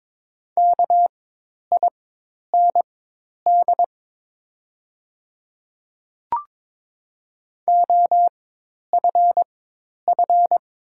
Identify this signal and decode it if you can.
Morse code, heard as a keyed tone, about 22 wpm, spaced as for 10 wpm, 700 Hz. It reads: KIND OFF
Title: 4x Character Spacing - Top 300 Words - 22wpm